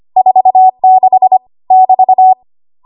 Morse code, heard as a keyed tone, about 25 wpm, 750 Hz. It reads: 46 <BT>
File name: Archive / 25 WPM / 25WPM